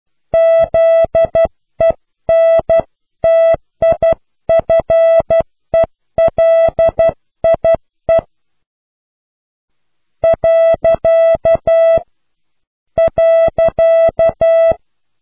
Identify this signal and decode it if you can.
Morse code, heard as a keyed tone, about 12 wpm, 660 Hz. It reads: ZENTIFELIE . .
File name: cw / pseudo worden(10)